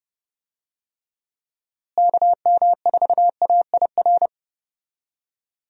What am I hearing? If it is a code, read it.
Morse code, heard as a keyed tone, about 30 wpm, 700 Hz. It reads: KM4AIR
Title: USA Call Signs - 6 Characters - Repeated 2x - 30wpm